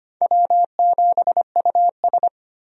Morse code, heard as a keyed tone, about 25 wpm, 700 Hz. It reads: W7US
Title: USA Call Signs - 4 Characters - 25wpm